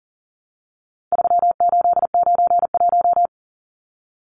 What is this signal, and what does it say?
Morse code, heard as a keyed tone, about 40 wpm, 700 Hz. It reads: 3891
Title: Numbers - 4-Digit 40wpm